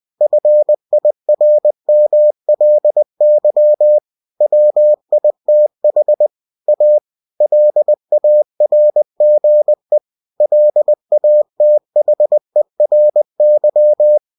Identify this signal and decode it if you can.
Morse code, heard as a keyed tone, about 20 wpm, 600 Hz. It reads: FIRMLY WITH A LARGE LATHERY
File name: cw-books / winnie-the-pooh / winnie-20wpm-page-0072